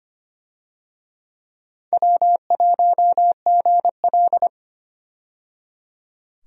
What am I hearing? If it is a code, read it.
Morse code, heard as a keyed tone, about 25 wpm, 700 Hz. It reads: W1GL
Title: USA Call Signs - 4 Characters - 25wpm